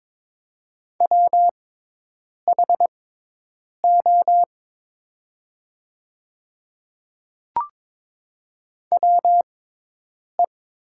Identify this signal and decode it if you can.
Morse code, heard as a keyed tone, about 22 wpm, spaced as for 8 wpm, 700 Hz. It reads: WHO WE
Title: 6x Character Spacing - Top 100 Words - 22wpm